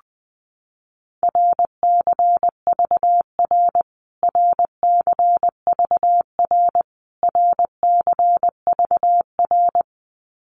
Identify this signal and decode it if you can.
Morse code, heard as a keyed tone, about 20 wpm, 700 Hz. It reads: RCVR RCVR RCVR
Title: Abbreviations Repeated 3x - 20wpm